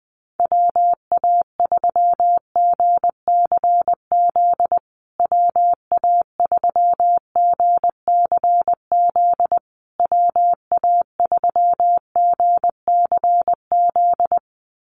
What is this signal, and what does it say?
Morse code, heard as a keyed tone, about 20 wpm, 700 Hz. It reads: WA3GCZ WA3GCZ WA3GCZ
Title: USA Call Signs Repeated 3x 20wpm